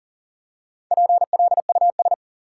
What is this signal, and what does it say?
Morse code, heard as a keyed tone, about 40 wpm, 700 Hz. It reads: PLUS
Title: Four Letter Words - 40wpm